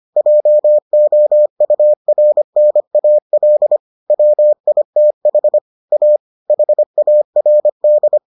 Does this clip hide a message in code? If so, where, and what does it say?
Yes: Morse code from 0.2 to 8.2 s, heard as a keyed tone, about 25 wpm, 600 Hz. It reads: JOURNAL WITH A HARD